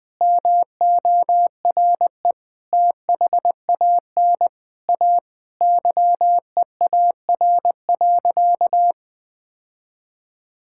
Morse code, heard as a keyed tone, about 20 wpm, 700 Hz. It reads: MORE THAN A YEAR.
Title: Sentences with 4-Word Phrases - 20wpm